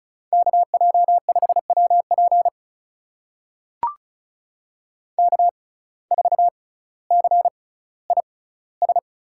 Morse code, heard as a keyed tone, about 36 wpm, spaced as for 18 wpm, 700 Hz. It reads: KJ5WP K4CIS
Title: USA Call Signs - 6x Character Spacing - 35wpm